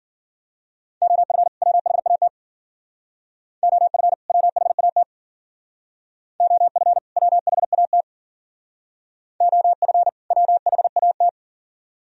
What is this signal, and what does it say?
Morse code, heard as a keyed tone, about 52 wpm, 700 Hz. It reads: OF WHAT OF WHAT OF WHAT OF WHAT